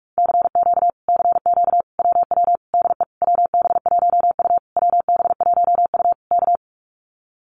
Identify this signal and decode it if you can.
Morse code, heard as a keyed tone, about 44 wpm, 700 Hz. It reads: CQ CQ WW DE WB1U WB1U K
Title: Common QSO Phrases Encore 45wpm